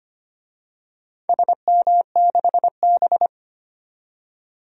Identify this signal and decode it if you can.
Morse code, heard as a keyed tone, about 25 wpm, 700 Hz. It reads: SM6B
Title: DXCC-100 Call Signs - 4 Characters - 25wpm